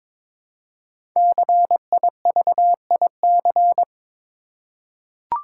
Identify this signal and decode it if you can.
Morse code, heard as a keyed tone, about 22 wpm, 690 Hz. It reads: CIVIC
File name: 5 Letter Words - First 3 Letters Given - 22wpm